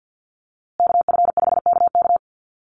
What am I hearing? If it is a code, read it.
Morse code, heard as a keyed tone, about 50 wpm, 700 Hz. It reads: KF5KK
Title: USA Call Signs Repeated 3x 50wpm